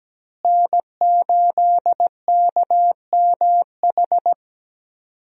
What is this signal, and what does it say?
Morse code, heard as a keyed tone, about 17 wpm, 700 Hz. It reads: N8KMH
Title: USA Call Signs - 5 Characters - Repeated 3x - 17wpm